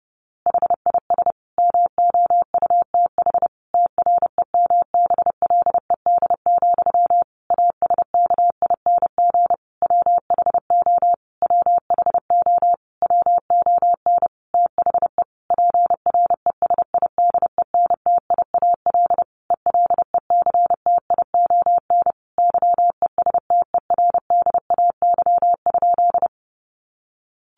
Morse code, heard as a keyed tone, about 30 wpm, 700 Hz. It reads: HIS MOUTH TREMBLED, ASKING WHO WHO WON THE PRESIDENTIAL ELECTION YESTERDAY?
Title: A Sound Of Thunder 30wpm